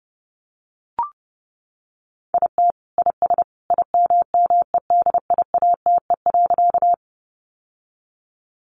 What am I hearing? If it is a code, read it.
Morse code, heard as a keyed tone, about 28 wpm, 700 Hz. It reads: IT IS IMMEDIATE.
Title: Sentences from Top 1200 Words 30wpm